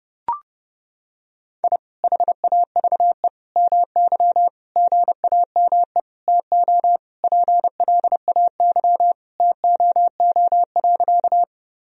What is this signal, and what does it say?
Morse code, heard as a keyed tone, about 30 wpm, 700 Hz. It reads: I HAVE MY GAME TO PLAY TOO.
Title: Sentences from Top 1000 Words - Review Part 3 - 30wpm